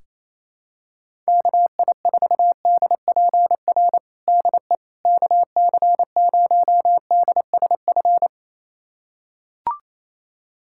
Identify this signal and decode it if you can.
Morse code, heard as a keyed tone, about 28 wpm, 700 Hz. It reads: KI4DPR DE KC0DSF
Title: USA Callsign DE Callsign 28wpm